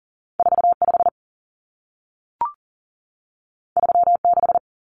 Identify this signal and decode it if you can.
Morse code, heard as a keyed tone, about 40 wpm, 700 Hz. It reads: 45 36
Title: Common Cut Numbers - 2-Digit 40wpm